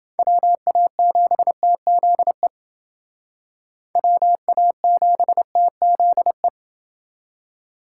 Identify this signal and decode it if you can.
Morse code, heard as a keyed tone, about 28 wpm, 700 Hz. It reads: WA7TZE WA7TZE